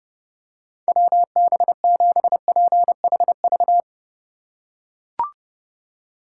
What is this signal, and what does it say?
Morse code, heard as a keyed tone, about 30 wpm, 700 Hz. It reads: WB7PHV